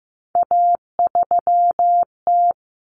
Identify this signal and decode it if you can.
Morse code, heard as a keyed tone, about 15 wpm, 700 Hz. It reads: A3T
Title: Common Cut Numbers - 3-Digit 15wpm